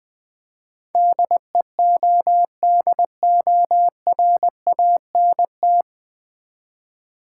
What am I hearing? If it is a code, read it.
Morse code, heard as a keyed tone, about 20 wpm, 700 Hz. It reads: DEODORANT
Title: Easy Words with Hints - 20wpm